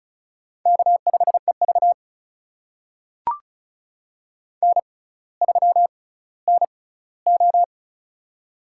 Morse code, heard as a keyed tone, about 36 wpm, spaced as for 18 wpm, 700 Hz. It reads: K5EV N3NO
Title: USA Call Signs - 4 Characters - 6x Character Spacing - 35wpm